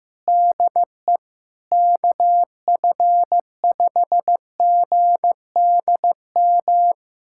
Morse code, heard as a keyed tone, about 16 wpm, 700 Hz. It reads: DE KF5GDM
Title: USA Callsign DE Callsign - 6 Characters - 15wpm